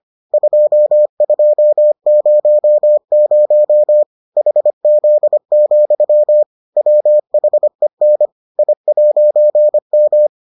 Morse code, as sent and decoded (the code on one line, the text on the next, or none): ..--- ..--- ----- ----- / .... --.. --..-- / .-- .... -. / .. .----. --
2200 HZ, WHEN I'M